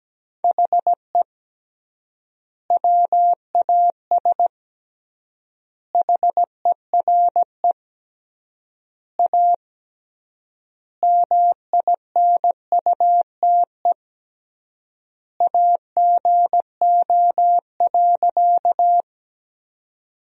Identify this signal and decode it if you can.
Morse code, heard as a keyed tone, about 17 wpm, 700 Hz. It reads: HE WAS HERE A MINUTE AGO.
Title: 3x Word-Spacing - Sentences from Top 500 Words - 17wpm